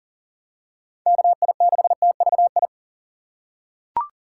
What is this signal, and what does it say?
Morse code, heard as a keyed tone, about 40 wpm, 700 Hz. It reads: KI6TVI
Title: USA Call Signs - 6 Characters - 3x Character Spacing - 40wpm